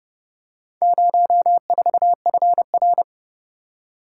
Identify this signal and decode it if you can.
Morse code, heard as a keyed tone, about 31 wpm, 700 Hz. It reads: 04FR